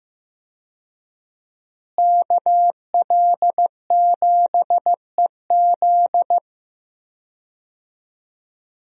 Morse code, heard as a keyed tone, about 15 wpm, 700 Hz. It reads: KL7EZ
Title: DXCC-100 Call Signs - 5 Characters - Repeated 3x - 15wpm